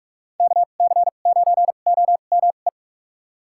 Morse code, heard as a keyed tone, about 44 wpm, spaced as for 32 wpm, 700 Hz. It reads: KC9OME
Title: USA Call Signs - 6 Characters - 2x Character Spacing - 45wpm